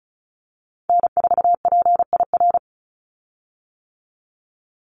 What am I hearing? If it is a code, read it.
Morse code, heard as a keyed tone, about 34 wpm, 700 Hz. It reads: N4PIR
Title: Call Signs - 35wpm